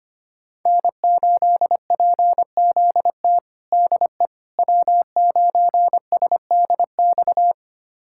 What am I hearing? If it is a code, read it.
Morse code, heard as a keyed tone, about 25 wpm, 700 Hz. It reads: N8PZT DE W9SDX
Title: USA Callsign DE Callsign - 5 Characters - 25wpm